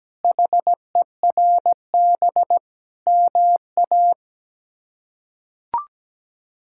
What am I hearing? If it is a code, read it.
Morse code, heard as a keyed tone, about 17 wpm, 700 Hz. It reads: HERB MA
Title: K1USN SST QSO Practice - 17wpm